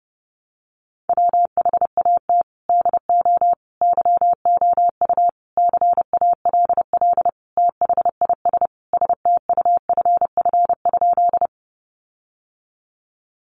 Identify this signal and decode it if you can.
Morse code, heard as a keyed tone, about 30 wpm, 700 Hz. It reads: WHAT DO YOU CALL THIS STUFF?
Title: Sentences from Top 1400 Words 30wpm